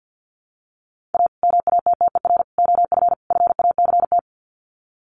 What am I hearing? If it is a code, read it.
Morse code, heard as a keyed tone, about 48 wpm, 700 Hz. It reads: A MATTER OF FACT
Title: Sets of 4 Words 50wpm